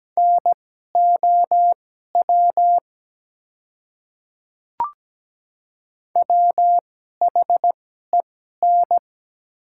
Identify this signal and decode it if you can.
Morse code, heard as a keyed tone, about 17 wpm, spaced as for 12 wpm, 700 Hz. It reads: NOW WHEN
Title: 2x Character Spacing - Top 300 Words - 17wpm